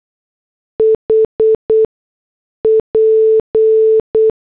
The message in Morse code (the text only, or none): HP